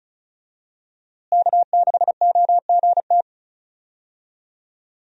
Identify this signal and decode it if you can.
Morse code, heard as a keyed tone, about 36 wpm, 700 Hz. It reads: K6OGT
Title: USA Call Signs - 5 Characters - Repeated 3x - 35wpm